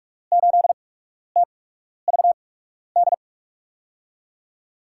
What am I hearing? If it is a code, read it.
Morse code, heard as a keyed tone, about 47 wpm, spaced as for 12 wpm, 700 Hz. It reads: 8TVD
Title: USA Call Signs - 8x Character Spacing - 45wpm